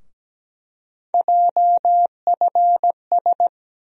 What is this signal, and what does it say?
Morse code, heard as a keyed tone, about 17 wpm, 700 Hz. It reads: JFS
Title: USA Call Sign Postfixes - 3 Letters 17wpm